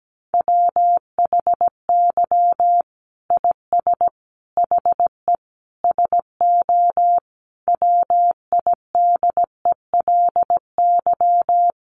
Morse code, heard as a keyed tone, about 17 wpm, 700 Hz. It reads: WHY IS HE SO WIDELY
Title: Sentences from Top 1200 Words 17wpm